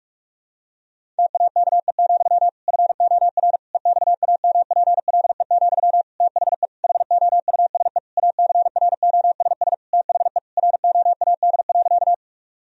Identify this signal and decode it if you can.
Morse code, heard as a keyed tone, about 44 wpm, 700 Hz. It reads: TAKE, FOR EXAMPLE, THE HOUSE ACROSS THE ROAD.
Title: Sentences from Top 1000 Words - Review Part 1 - 45wpm